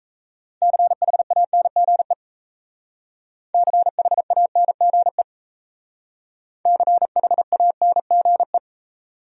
Morse code, heard as a keyed tone, about 36 wpm, 700 Hz. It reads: CHANGE CHANGE CHANGE